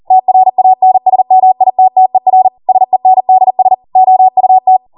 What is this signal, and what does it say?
Morse code, heard as a keyed tone, about 40 wpm, 750 Hz. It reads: TRANSMITTER SENDS OUT